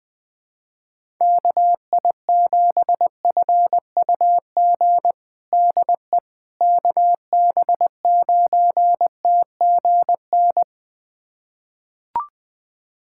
Morse code, heard as a keyed tone, about 21 wpm, 700 Hz. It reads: KI7FUG DE KB9TGN
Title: USA Callsign DE Callsign - 6 Characters - 20wpm